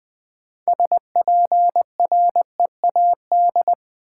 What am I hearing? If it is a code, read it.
Morse code, heard as a keyed tone, about 20 wpm, 700 Hz. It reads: SPREAD
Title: Sentences from Top 1200 Words - New Word Drill Alt - 20wpm